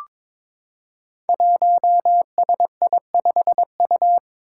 Morse code, heard as a keyed tone, about 22 wpm, 700 Hz. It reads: 1SI5U
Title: Five Challenging Letter-Number Combinations 22wpm